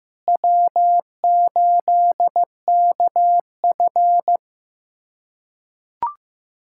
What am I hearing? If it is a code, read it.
Morse code, heard as a keyed tone, about 16 wpm, 700 Hz. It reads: W8KF
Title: USA Callsign DE Callsign - 4 Characters - 15wpm